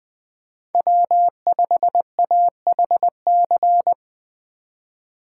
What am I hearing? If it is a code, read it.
Morse code, heard as a keyed tone, about 23 wpm, 700 Hz. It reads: W5AHC
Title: Speed Racing - USA 1x3 Call Signs 20wpm